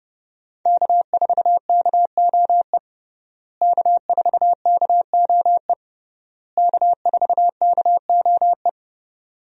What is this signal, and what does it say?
Morse code, heard as a keyed tone, about 32 wpm, 700 Hz. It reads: K4KOE K4KOE K4KOE
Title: USA Call Signs - 5 Characters - Repeated 3x - 30wpm